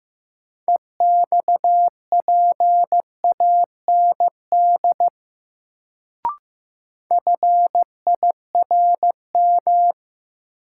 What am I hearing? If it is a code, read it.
Morse code, heard as a keyed tone, about 15 wpm, 700 Hz. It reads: EXPAND FIRM
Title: Sentences from Top 1400 Words - New Word Drill - 15wpm